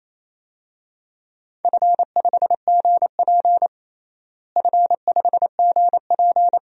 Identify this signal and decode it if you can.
Morse code, heard as a keyed tone, about 28 wpm, 700 Hz. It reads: F5GP F5GP
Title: DXCC-100 Call Signs - 4 Characters - Repeated 2x - 28wpm